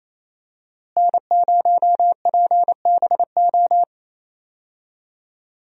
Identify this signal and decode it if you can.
Morse code, heard as a keyed tone, about 28 wpm, 700 Hz. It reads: N0PBO
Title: USA Call Signs - 5 Characters - Repeated 2x - 28wpm